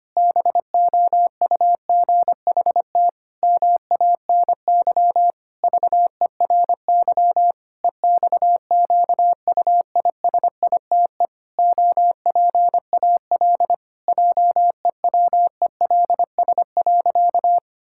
Morse code, heard as a keyed tone, about 25 wpm, 700 Hz. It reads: BOUGHT MANY VERY EXQUISITE OPAL JEWELS.
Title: Pangram Challenge v2 - 25wpm